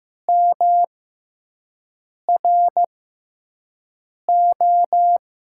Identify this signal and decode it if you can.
Morse code, heard as a keyed tone, about 15 wpm, spaced as for 5 wpm, 700 Hz. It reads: MRO
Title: USA Call Signs - 5 Characters - 6x Character Spacing - 15wpm